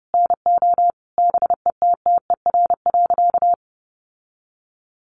Morse code, heard as a keyed tone, about 30 wpm, 700 Hz. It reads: NO BETTER.
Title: Sentences from Top 1300 Words 30wpm